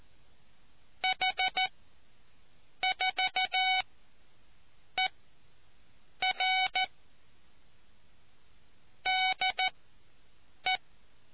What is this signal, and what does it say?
Morse code, heard as a keyed tone, about 14 wpm, spaced as for 7 wpm, 750 Hz. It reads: H 4 E R DE